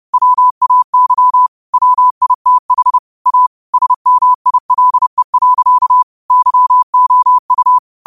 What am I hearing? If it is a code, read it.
Morse code, heard as a keyed tone, about 30 wpm, 1000 Hz. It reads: WAY WITH A SMILE. YOU
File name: qotdfiles / 30wpm